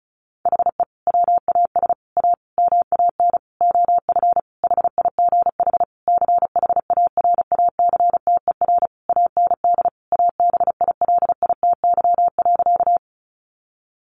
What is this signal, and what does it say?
Morse code, heard as a keyed tone, about 34 wpm, 700 Hz. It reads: HE WAS A MAN OF HIGH CHARACTER AND ABILITY.
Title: Sentences from Top 1200 Words 35wpm